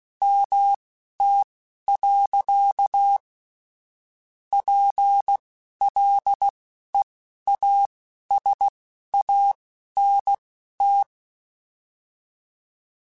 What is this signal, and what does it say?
Morse code, heard as a keyed tone, about 16 wpm, spaced as for 11 wpm, 800 Hz. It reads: MT. PLEASANT